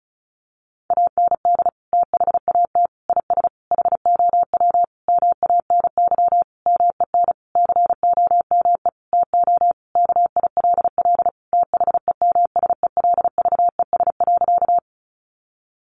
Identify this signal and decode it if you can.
Morse code, heard as a keyed tone, about 34 wpm, 700 Hz. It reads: AND THAT IS HOW MANY MEN COME TO KILL THEMSELVES.
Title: Sentences from Top 1400 Words 35wpm